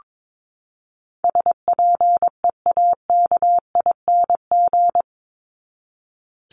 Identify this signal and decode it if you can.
Morse code, heard as a keyed tone, about 22 wpm, 700 Hz. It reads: SPEAKING
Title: ING Suffix - Top 100 Words with Base Words 22wpm